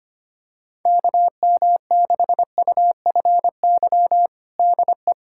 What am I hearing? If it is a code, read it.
Morse code, heard as a keyed tone, about 25 wpm, 700 Hz. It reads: KM6UFY DE
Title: USA Callsign DE Callsign 25wpm